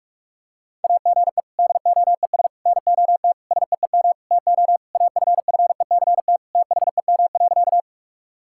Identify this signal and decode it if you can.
Morse code, heard as a keyed tone, about 46 wpm, 700 Hz. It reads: AGE DOES NOT SEEM TO AFFECT THEM.
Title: Sentences from Top 1000 Words - Review Part 6 - 45wpm